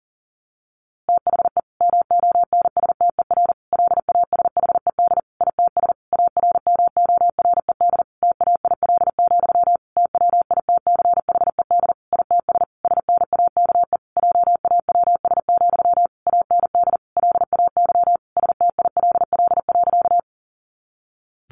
Morse code, heard as a keyed tone, about 40 wpm, 700 Hz. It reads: THE MONSTER LASHED ITS ARMORED TAIL, TWITCHED ITS SNAKE JAWS, AND LAY STILL.
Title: A Sound Of Thunder 40wpm